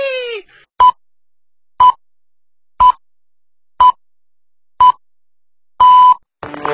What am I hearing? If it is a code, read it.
Radio time signal (time pips). It is heard from 0.8 to 6.1 s.